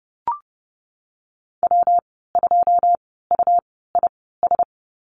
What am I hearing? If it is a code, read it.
Morse code, heard as a keyed tone, about 30 wpm, spaced as for 17 wpm, 700 Hz. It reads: W2UIS
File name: Lesson 19g - Letter w - Call Signs 30wpm